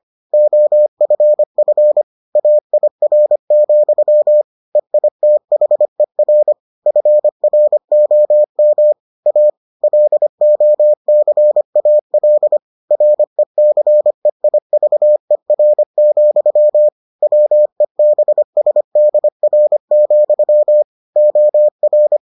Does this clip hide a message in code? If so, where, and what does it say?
Yes: Morse code from 0.3 to 22.2 s, heard as a keyed tone, about 25 wpm, 600 Hz. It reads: OFF AIR, EITHER FROM A LOCAL RECEIVER, WEBSDR, OR